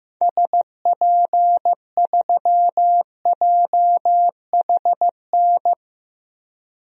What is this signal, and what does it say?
Morse code, heard as a keyed tone, about 15 wpm, 700 Hz. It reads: SP3JHN
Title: DXCC-100 Call Signs - 6 Characters - Repeated 2x - 15wpm